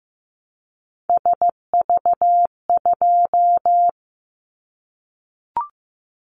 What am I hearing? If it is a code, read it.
Morse code, heard as a keyed tone, about 15 wpm, 700 Hz. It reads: SV2
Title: Callsign Prefixes for Top 50 DXCC - 15wpm